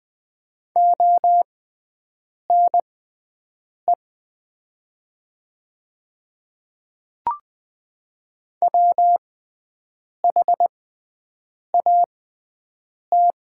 Morse code, heard as a keyed tone, about 20 wpm, spaced as for 7 wpm, 700 Hz. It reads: ONE WHAT
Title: 6x Character Spacing - Top 100 Words - 20wpm